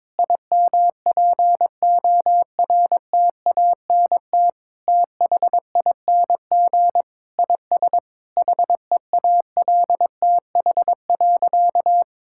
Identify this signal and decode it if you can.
Morse code, heard as a keyed tone, about 22 wpm, 700 Hz. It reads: IMPORTANT THING IS HEALTH.
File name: practice / 22wpm / Sentences with 3-Word Phrases - 22wpm